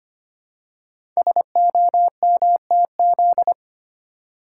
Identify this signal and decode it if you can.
Morse code, heard as a keyed tone, about 25 wpm, 700 Hz. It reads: SOMTZ
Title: Code Groups - 5 Letter:Numbers - 25wpm